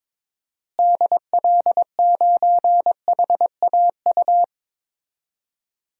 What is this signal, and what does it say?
Morse code, heard as a keyed tone, about 21 wpm, 700 Hz. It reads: DL9HAU
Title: DXCC-100 Call Signs - 6 Characters - Repeated 3x - 22wpm